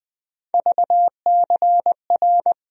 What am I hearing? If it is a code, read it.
Morse code, heard as a keyed tone, about 20 wpm, 700 Hz. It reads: VCR
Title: USA Call Sign Postfixes - 3 Letters 20wpm